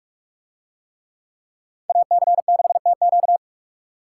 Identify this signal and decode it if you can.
Morse code, heard as a keyed tone, about 44 wpm, 690 Hz. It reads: AC6TQ